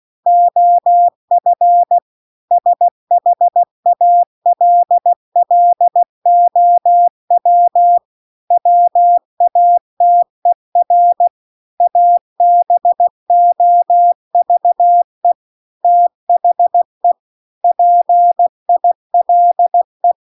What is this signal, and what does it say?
Morse code, heard as a keyed tone, about 16 wpm, 700 Hz. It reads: OF SHALLOW WATER ABOVE THE PILE
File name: CW-The-Big-Twoheaded-River-2-16WPM